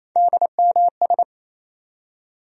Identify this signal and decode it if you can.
Morse code, heard as a keyed tone, about 28 wpm, 700 Hz. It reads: DMS
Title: USA Call Sign Postfixes - 3 Letters 28wpm